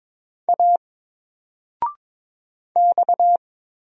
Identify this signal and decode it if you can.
Morse code, heard as a keyed tone, about 22 wpm, 700 Hz. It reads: A X
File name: Single-Letter-Number-22wpm